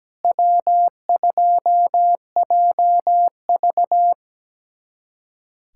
Morse code, heard as a keyed tone, about 17 wpm, 700 Hz. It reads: W2JV
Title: USA Callsign DE Callsign - 4 Characters - 17wpm